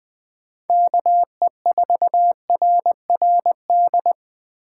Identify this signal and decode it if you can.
Morse code, heard as a keyed tone, about 20 wpm, 700 Hz. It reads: KE4RRD
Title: Speed Racing - USA 2x3 Call Signs 17wpm